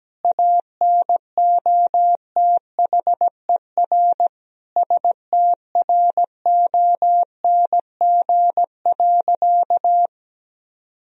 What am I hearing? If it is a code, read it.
Morse code, heard as a keyed tone, about 17 wpm, 700 Hz. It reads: ANOTHER STRONG.
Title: Sentences from Top 1000 Words - Review Part 3 - 17wpm